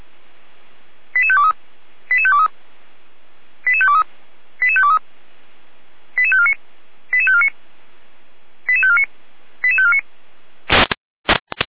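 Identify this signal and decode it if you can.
Five-tone selective call (ZVEI), heard as quick runs of five tones, tones of about 80 ms. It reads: 88523 88523 88523 88523 88549 88549 88549 88549
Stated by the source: ZVEI five-tone call received by SDR.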